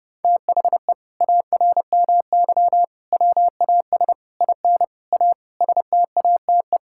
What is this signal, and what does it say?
Morse code, heard as a keyed tone, about 30 wpm, 700 Hz. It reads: THE ARMY WAS IN A STATE
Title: Sentences from Top 1600 Words 30wpm